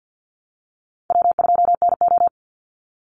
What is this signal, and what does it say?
Morse code, heard as a keyed tone, about 50 wpm, 700 Hz. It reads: W2NO
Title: Call Signs - 50wpm